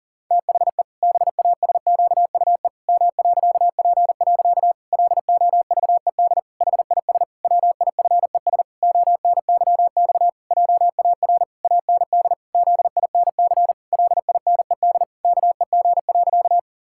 Morse code, heard as a keyed tone, about 37 wpm, 700 Hz. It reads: THE BASQUE M.P. LOVED HIS WIFES ONYX JAR AND ZINC LINED KEG.
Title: Pangram Challenge v2 - 40wpm